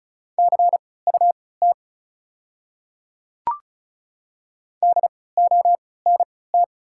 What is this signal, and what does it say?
Morse code, heard as a keyed tone, about 36 wpm, spaced as for 20 wpm, 700 Hz. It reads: CUT DONT
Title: 3x Character Spacing - Top 300 Words - 35wpm